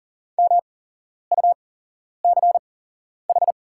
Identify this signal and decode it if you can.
Morse code, heard as a keyed tone, about 39 wpm, spaced as for 11 wpm, 700 Hz. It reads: MUCH
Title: 8x Character Spacing - Top 300 Words - 40wpm